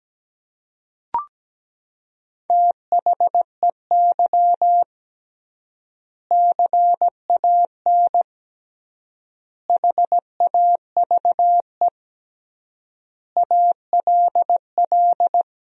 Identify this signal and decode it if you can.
Morse code, heard as a keyed tone, about 17 wpm, 700 Hz. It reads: THEY CAN HAVE ALL
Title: 3x Word-Spacing - Sentences from Top 400 Words - 17wpm